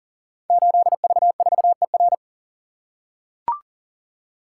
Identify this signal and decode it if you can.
Morse code, heard as a keyed tone, about 38 wpm, 700 Hz. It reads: 8V4ER
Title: Code Groups - 5 Letter:Numbers - 40wpm